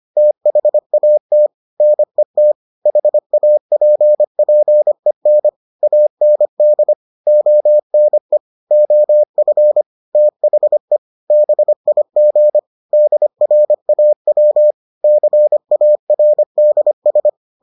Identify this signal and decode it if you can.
Morse code, heard as a keyed tone, about 25 wpm, 600 Hz. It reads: THAT NET HAPPEN AND ONE OF THE BIG DRAW CARDS